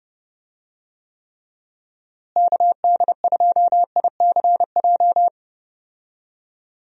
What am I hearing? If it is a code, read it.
Morse code, heard as a keyed tone, about 30 wpm, 700 Hz. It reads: KD2ICJ
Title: USA Call Signs - 6 Characters - Repeated 3x - 30wpm